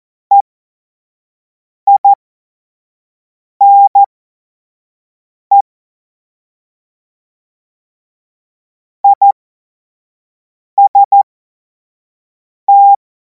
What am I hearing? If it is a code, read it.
Morse code, heard as a keyed tone, about 14 wpm, spaced as for 5 wpm, 800 Hz. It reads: EINE IST
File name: Morsekurs / Witze / Witz 03. Februar 2026 25 F70